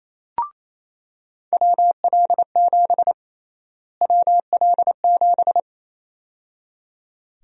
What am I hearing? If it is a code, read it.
Morse code, heard as a keyed tone, about 27 wpm, 700 Hz. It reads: WL7 WL7
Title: USA Call Sign Prefixes - Repeated 2x - 28wpm